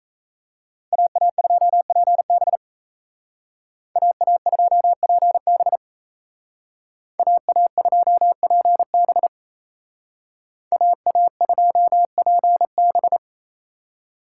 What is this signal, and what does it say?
Morse code, heard as a keyed tone, about 37 wpm, 700 Hz. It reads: AA2PB AA2PB AA2PB AA2PB